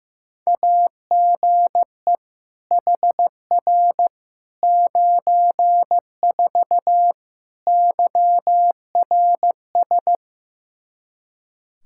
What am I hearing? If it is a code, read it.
Morse code, heard as a keyed tone, about 16 wpm, 700 Hz. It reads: AGE HR 94 YRS
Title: K1USN SST QSO Practice 15wpm